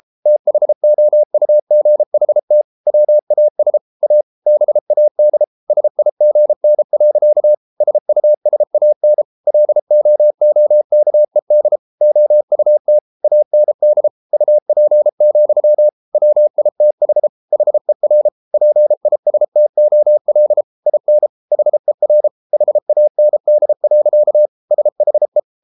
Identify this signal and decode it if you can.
Morse code, heard as a keyed tone, about 33 wpm, 600 Hz. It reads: THOUGHT WAS A BAD SIGN. SUSAN LOOKED OUT AND UP, WITH HER PISTOL IN HER HAND. SHE